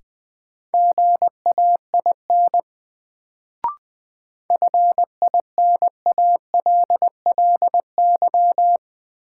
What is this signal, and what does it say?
Morse code, heard as a keyed tone, about 20 wpm, 700 Hz. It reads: GAIN FINALLY
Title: Sentences from Top 1000 Words - New Word Drill - 20wpm